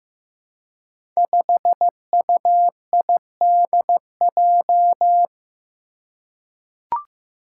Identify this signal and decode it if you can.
Morse code, heard as a keyed tone, about 15 wpm, 700 Hz. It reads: 5UIDJ
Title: Five Challenging Letter-Number Combinations 15wpm